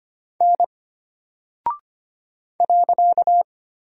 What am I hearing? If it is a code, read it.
Morse code, heard as a keyed tone, about 25 wpm, 700 Hz. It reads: N .
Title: Single Character (Letter:Number:Punctuation:Prosign) Alt 25wpm